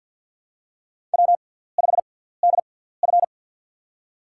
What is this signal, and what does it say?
Morse code, heard as a keyed tone, about 53 wpm, spaced as for 17 wpm, 700 Hz. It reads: W5DF
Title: USA Call Signs - 4 Characters - 6x Character Spacing - 50wpm